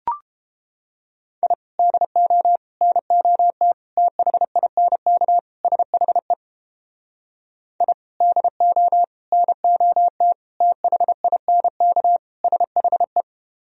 Morse code, heard as a keyed tone, about 29 wpm, 700 Hz. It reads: I DO NOT THINK SHE I DO NOT THINK SHE